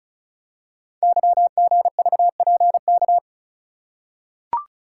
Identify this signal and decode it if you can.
Morse code, heard as a keyed tone, about 36 wpm, 700 Hz. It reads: YGVPK
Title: Code Groups - 5 Letters - 35wpm